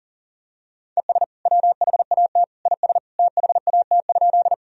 Morse code, heard as a keyed tone, about 40 wpm, 700 Hz. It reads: ES WHAT IS THAT?
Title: Sentences from Top 100 Words and Abbreviations 40wpm